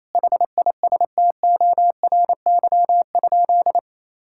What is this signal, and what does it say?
Morse code, heard as a keyed tone, about 28 wpm, 700 Hz. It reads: HISTORY?
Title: Sentences from Top 1000 Words - Review Part 5 - 28wpm